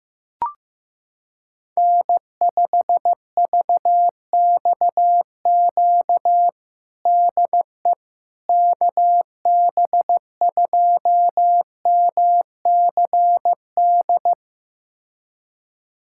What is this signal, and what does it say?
Morse code, heard as a keyed tone, about 15 wpm, 700 Hz. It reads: N5VXQ DE KB2MCD
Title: USA Callsign DE Callsign 15wpm